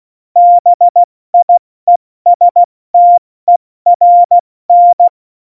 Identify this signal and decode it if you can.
Morse code, heard as a keyed tone, about 16 wpm, 700 Hz. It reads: BIESTERN
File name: Morsekurs / Witze / Witz 19. Dezember 2017, 70 F80